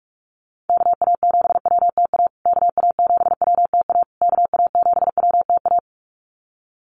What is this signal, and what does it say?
Morse code, heard as a keyed tone, about 44 wpm, 690 Hz. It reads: KA7WTA KA7WTA KA7WTA
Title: USA Call Signs Repeated 3x 45wpm